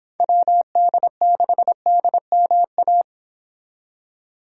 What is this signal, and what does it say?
Morse code, heard as a keyed tone, about 25 wpm, 700 Hz. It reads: WD6DMA